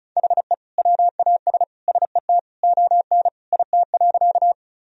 Morse code, heard as a keyed tone, about 34 wpm, 700 Hz. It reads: HE WAS SET ON IT.
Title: Sentences from Top 1000 Words - Review Part 1 - 35wpm